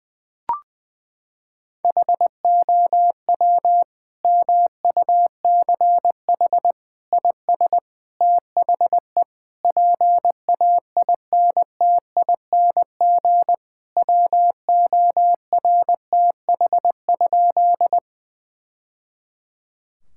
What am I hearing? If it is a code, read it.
Morse code, heard as a keyed tone, about 21 wpm, 700 Hz. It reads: HOW MUCH IS THE PAINTING WORTH?
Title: Sentences from Top 1000 Words - Review Part 6 - 20wpm